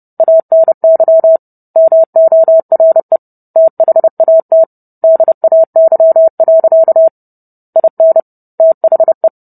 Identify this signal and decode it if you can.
Morse code, heard as a keyed tone, about 30 wpm, 650 Hz. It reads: ANY MORE THAT DAY. IN THE